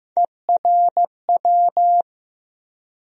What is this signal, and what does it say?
Morse code, heard as a keyed tone, about 15 wpm, 700 Hz. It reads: ERW
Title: USA Call Sign Postfixes - 3 Letters 15wpm